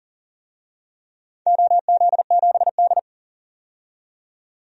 Morse code, heard as a keyed tone, about 40 wpm, 700 Hz. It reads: OZ7D
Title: DXCC-100 Call Signs - 4 Characters - 40wpm